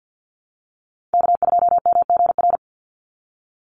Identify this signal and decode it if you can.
Morse code, heard as a keyed tone, about 50 wpm, 700 Hz. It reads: K2MGR
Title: Call Signs Repeated 3x - 50wpm